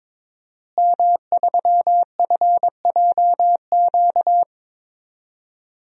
Morse code, heard as a keyed tone, about 21 wpm, 700 Hz. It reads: M3FJQ
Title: DXCC-100 Call Signs - 5 Characters - Repeated 2x - 22wpm